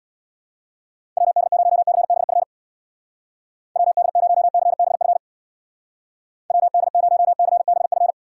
Morse code, heard as a keyed tone, about 62 wpm, 700 Hz. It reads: WD0CBF WD0CBF WD0CBF
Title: Speed Racing - USA Call Signs 50wpm